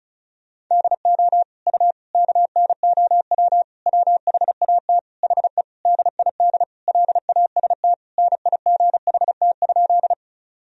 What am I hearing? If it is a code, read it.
Morse code, heard as a keyed tone, about 36 wpm, 700 Hz. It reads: DO U KNOW WHAT HE DID LAST NIGHT?
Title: Sentences from Top 300 Words and Abbreviations 35wpm